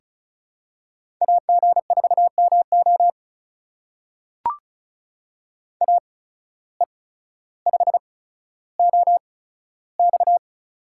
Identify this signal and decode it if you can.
Morse code, heard as a keyed tone, about 36 wpm, spaced as for 15 wpm, 700 Hz. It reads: AG4MO AE5OX
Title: USA Call Signs - 8x Character Spacing - 35wpm